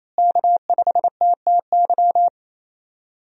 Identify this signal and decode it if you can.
Morse code, heard as a keyed tone, about 28 wpm, 700 Hz. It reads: K5TTY
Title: USA Call Signs - 5 Characters - Repeated 3x - 28wpm